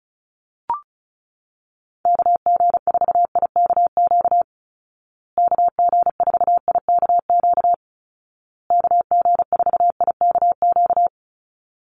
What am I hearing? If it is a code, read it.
Morse code, heard as a keyed tone, about 34 wpm, 700 Hz. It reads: KG4IKQ KG4IKQ KG4IKQ
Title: Call Signs Repeated 2x - 35wpm